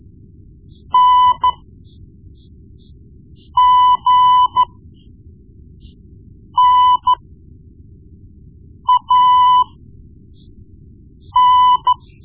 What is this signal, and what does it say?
Morse code, heard as a keyed tone, about 13 wpm, spaced as for 4 wpm, 1000 Hz. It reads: NGNAN
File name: Morse Lessons / Morse Lessons / Tape-1B